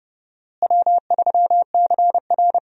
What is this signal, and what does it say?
Morse code, heard as a keyed tone, about 31 wpm, 700 Hz. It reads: W3CR
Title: USA Call Signs - 4 Characters - Repeated 2x - 30wpm